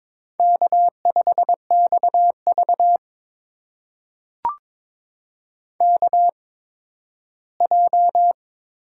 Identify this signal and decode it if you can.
Morse code, heard as a keyed tone, about 22 wpm, spaced as for 13 wpm, 700 Hz. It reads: K5XV KJ
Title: USA Call Signs - 8x Character Spacing - 22wpm